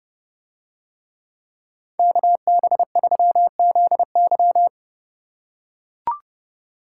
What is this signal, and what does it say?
Morse code, heard as a keyed tone, about 29 wpm, 700 Hz. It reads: KB3ZY